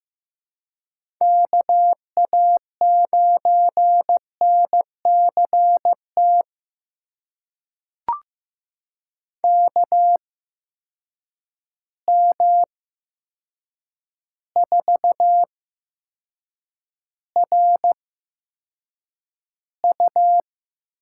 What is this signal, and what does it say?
Morse code, heard as a keyed tone, about 15 wpm, spaced as for 7 wpm, 700 Hz. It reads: KA9NCT KM4RU
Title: USA Call Signs - 6 Characters - 8x Character Spacing - 15wpm